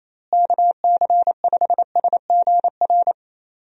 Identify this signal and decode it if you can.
Morse code, heard as a keyed tone, about 28 wpm, 700 Hz. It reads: KC5SGR
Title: USA Call Signs - 6 Characters - Repeated 3x - 28wpm